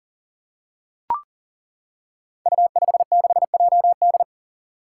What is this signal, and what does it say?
Morse code, heard as a keyed tone, about 39 wpm, 700 Hz. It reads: U56JD
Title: Five Challenging Letter-Number Combinations 40wpm